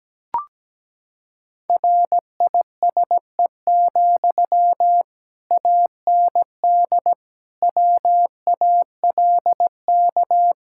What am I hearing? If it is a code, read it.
Morse code, heard as a keyed tone, about 17 wpm, 700 Hz. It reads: RISE, AND WALK
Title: Sentences from Top 1000 Words - Review Part 6 - 17wpm